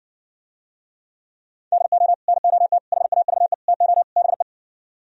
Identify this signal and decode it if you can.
Morse code, heard as a keyed tone, about 69 wpm, 700 Hz. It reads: DO NOT HAVE TO BE